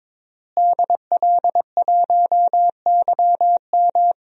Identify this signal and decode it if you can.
Morse code, heard as a keyed tone, about 22 wpm, 700 Hz. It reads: DL1YM
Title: DXCC-100 Call Signs - 5 Characters - Repeated 2x - 22wpm